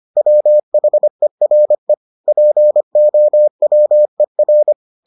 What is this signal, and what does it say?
Morse code, heard as a keyed tone, about 25 wpm, 600 Hz. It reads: WHERE POWER